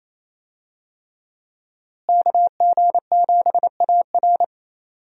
Morse code, heard as a keyed tone, about 28 wpm, 700 Hz. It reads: KG7AR
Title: USA Call Signs - 5 Characters - Repeated 3x - 28wpm